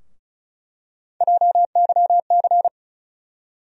Morse code, heard as a keyed tone, about 36 wpm, 700 Hz. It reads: JYC
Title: USA Call Sign Postfixes - 3 Letters 35wpm